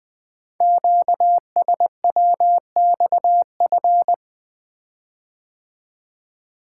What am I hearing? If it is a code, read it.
Morse code, heard as a keyed tone, about 20 wpm, 700 Hz. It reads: QSWXF
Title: Code Groups - 5 Letters - 20wpm